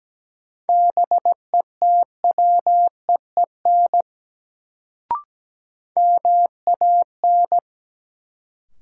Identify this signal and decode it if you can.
Morse code, heard as a keyed tone, about 17 wpm, 700 Hz. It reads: BETWEEN MAN
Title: Sets of 2 Words - New Word Drill - 17wpm